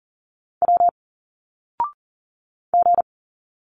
Morse code, heard as a keyed tone, about 40 wpm, 700 Hz. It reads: W G
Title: Single Letters 40wpm